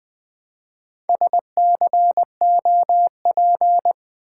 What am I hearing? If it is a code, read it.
Morse code, heard as a keyed tone, about 21 wpm, 700 Hz. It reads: SCOP
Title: Build A Word - 1 Letter At A Time - 3 to 5-Letter Words - 20wpm